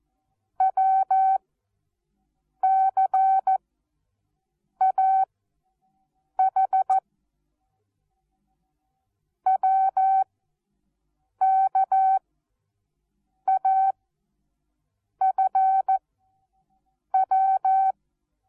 Morse code, heard as a keyed tone, about 14 wpm, 770 Hz. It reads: W C A H W K A F W
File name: Onlinelehrgang / morsen / DJ4UF Morsen19